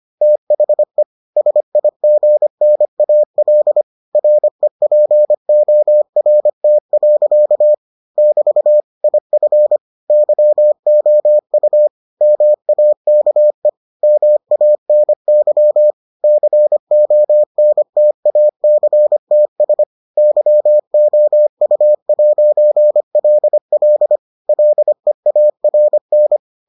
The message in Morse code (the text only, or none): THE SIGNAL REPORT. <BT> IF YOU MAKE MANY CONTACTS YOU'LL LEARN